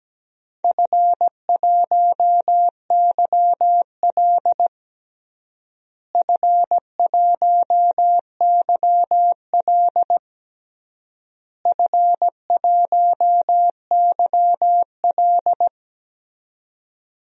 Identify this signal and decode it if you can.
Morse code, heard as a keyed tone, about 17 wpm, 700 Hz. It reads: F1YL F1YL F1YL
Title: DXCC-100 Call Signs - 4 Characters - Repeated 3x - 17wpm